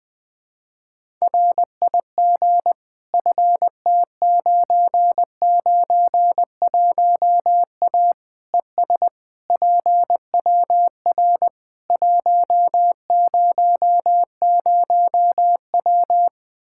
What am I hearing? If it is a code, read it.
Morse code, heard as a keyed tone, about 20 wpm, 700 Hz. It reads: RIG FT991A ES PWR 100W
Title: Practice QSO - Exchange 2 - Info - 20wpm